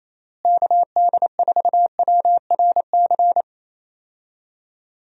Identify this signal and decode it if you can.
Morse code, heard as a keyed tone, about 28 wpm, 700 Hz. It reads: KD4WRC
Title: USA Call Signs - 6 Characters - Repeated 3x - 28wpm